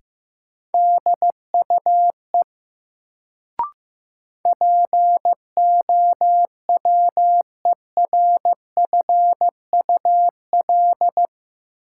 Morse code, heard as a keyed tone, about 15 wpm, 700 Hz. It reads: DUE POWERFUL
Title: Sets of 5 Words Encore - New Word Drill - 15wpm